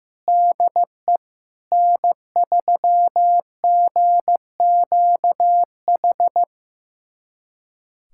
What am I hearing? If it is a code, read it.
Morse code, heard as a keyed tone, about 15 wpm, 700 Hz. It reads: DE N3GQH
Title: USA Callsign DE Callsign - 5 Characters - 15wpm